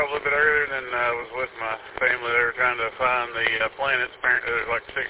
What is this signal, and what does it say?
Emergency Alert System (SAME) tones (header bursts), heard as harsh buzzing bursts, with speech. It is heard from 0.2 to 4.5 s.
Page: w8um-145230